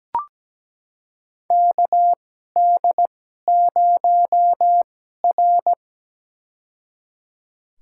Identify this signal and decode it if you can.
Morse code, heard as a keyed tone, about 17 wpm, spaced as for 12 wpm, 700 Hz. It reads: KD0R
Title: USA Call Signs - 4 Characters - 2x Character Spacing - 17wpm